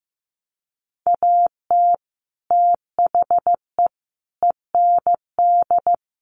Morse code, heard as a keyed tone, about 15 wpm, 700 Hz. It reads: AT THE END
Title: Sets of 5 Words 15wpm